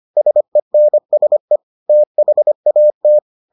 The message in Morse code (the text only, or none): SENSE THAT